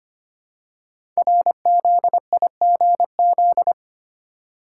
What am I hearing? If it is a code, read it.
Morse code, heard as a keyed tone, about 25 wpm, 700 Hz. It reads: RZIGZ
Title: Code Groups - 5 Letters - 25wpm